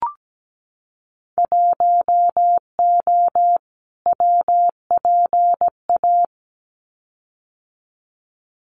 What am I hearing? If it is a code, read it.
Morse code, heard as a keyed tone, about 17 wpm, 700 Hz. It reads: 1O WPA
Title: Winter Field Day Exchanges 17wpm